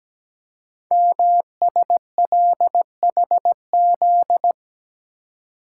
Morse code, heard as a keyed tone, about 17 wpm, 700 Hz. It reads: MSLHZ
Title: Code Groups - 5 Letter:Numbers - 17wpm